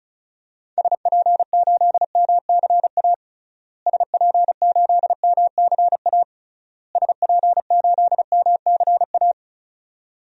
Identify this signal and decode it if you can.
Morse code, heard as a keyed tone, about 36 wpm, 700 Hz. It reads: SP8MCA SP8MCA SP8MCA
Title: DXCC-100 Call Signs - 6 Characters - Repeated 3x - 35wpm